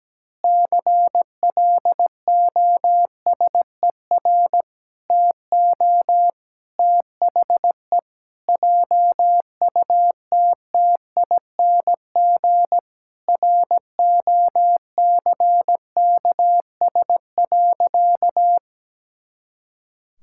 Morse code, heard as a keyed tone, about 17 wpm, 700 Hz. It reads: CLOSER TO THE JUTTING ROCKS.
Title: Pangram Challenge v2 - 17wpm